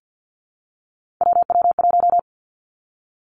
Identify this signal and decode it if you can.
Morse code, heard as a keyed tone, about 50 wpm, 700 Hz. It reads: WW1
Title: Three Challenging Letter-Number Combinations 50wpm